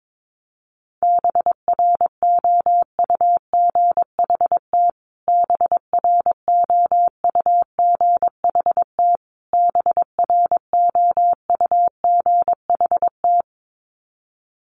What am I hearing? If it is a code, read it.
Morse code, heard as a keyed tone, about 22 wpm, 700 Hz. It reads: BROUGHT BROUGHT BROUGHT
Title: Top 500 Words Repeated 22wpm